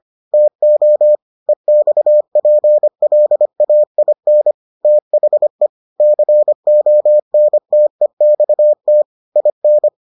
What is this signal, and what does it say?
Morse code, heard as a keyed tone, about 25 wpm, 600 Hz. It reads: TO EXPLAIN THE CONTEXT IN